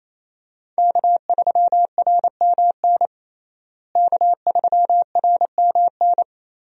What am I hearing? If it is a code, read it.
Morse code, heard as a keyed tone, about 28 wpm, 700 Hz. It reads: K3RMN K3RMN
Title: USA Call Signs - 5 Characters - Repeated 3x - 28wpm